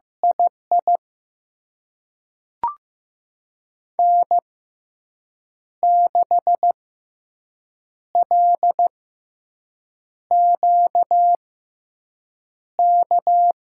Morse code, heard as a keyed tone, about 15 wpm, spaced as for 6 wpm, 700 Hz. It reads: II N6LQK
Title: USA Call Signs - 6x Character Spacing - 15wpm